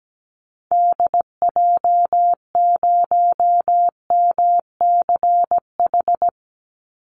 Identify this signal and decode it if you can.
Morse code, heard as a keyed tone, about 17 wpm, 700 Hz. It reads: DJ0MCH
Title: Call Signs Repeated 2x - 17wpm